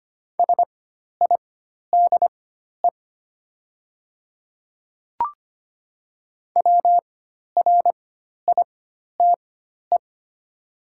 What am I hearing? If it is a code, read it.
Morse code, heard as a keyed tone, about 25 wpm, spaced as for 12 wpm, 700 Hz. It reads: SIDE WRITE
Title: 4x Character Spacing - Top 100 Words - 25wpm